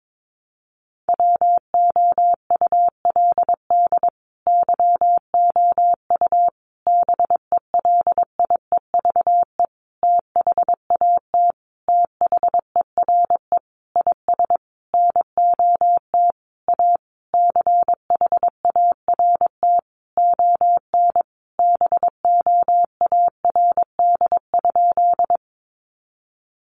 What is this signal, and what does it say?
Morse code, heard as a keyed tone, about 22 wpm, 700 Hz. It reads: WOULD YOU BELIEVE THAT THERE IS NOT A CHART ON BOARD?
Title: Sentences from Top 1300 Words 22wpm